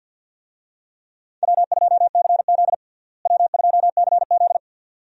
Morse code, heard as a keyed tone, about 53 wpm, 700 Hz. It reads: W2CZ W2CZ
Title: USA Call Signs - 4 Characters - Repeated 2x - 50wpm